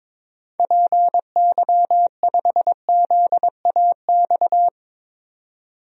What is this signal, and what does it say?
Morse code, heard as a keyed tone, about 22 wpm, 700 Hz. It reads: PY5ZAX
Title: DXCC-100 Call Signs - 6 Characters - 22wpm